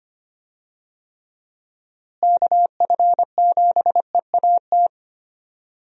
Morse code, heard as a keyed tone, about 25 wpm, 700 Hz. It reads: KF7EAT
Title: USA Call Signs - 6 Characters - Repeated 3x - 25wpm